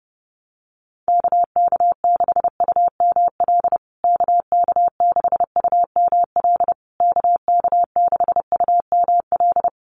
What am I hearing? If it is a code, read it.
Morse code, heard as a keyed tone, about 30 wpm, 700 Hz. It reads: KK6UML KK6UML KK6UML
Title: USA Call Signs Repeated 3x 30wpm